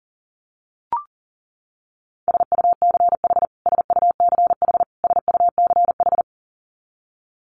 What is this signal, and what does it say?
Morse code, heard as a keyed tone, about 38 wpm, 700 Hz. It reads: SUCH SUCH SUCH
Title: Top 300 Words Repeated 40wpm